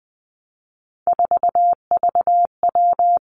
Morse code, heard as a keyed tone, about 20 wpm, 700 Hz. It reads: 4VW
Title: Three Challenging Letter-Number Combinations 20wpm